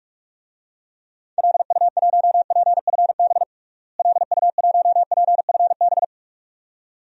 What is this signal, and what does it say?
Morse code, heard as a keyed tone, about 42 wpm, 700 Hz. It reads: LU1PFB LU1PFB
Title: DXCC-100 Call Signs - 6 Characters - Repeated 2x - 45wpm